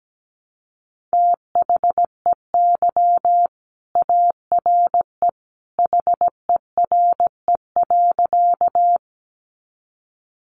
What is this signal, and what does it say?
Morse code, heard as a keyed tone, about 17 wpm, 700 Hz. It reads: THEY ARE HERE.
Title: Sentences from Top 200 Words 17wpm